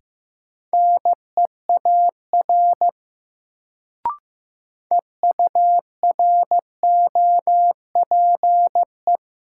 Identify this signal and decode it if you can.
Morse code, heard as a keyed tone, about 15 wpm, 700 Hz. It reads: NEAR EUROPE
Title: Sets of 5 Words Encore - New Word Drill - 15wpm